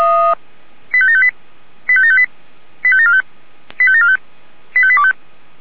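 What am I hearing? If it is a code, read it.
Five-tone selective call (ZVEI), heard as quick runs of five tones, tones of about 70 ms. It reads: 85758 85758 85745 85745 85825
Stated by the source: ZVEI five-tone call 85725 received by SDR.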